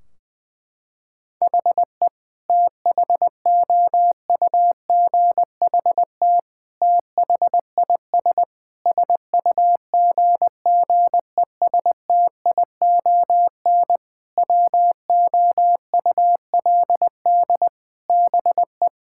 Morse code, heard as a keyed tone, about 20 wpm, 700 Hz. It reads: HE THOUGHT THIS SUGGESTION WOULD BE
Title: Sentences from Top 1600 Words 20wpm